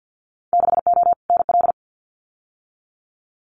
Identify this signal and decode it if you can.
Morse code, heard as a keyed tone, about 50 wpm, 700 Hz. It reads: 6O NL
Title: Winter Field Day Exchanges Encore 50wpm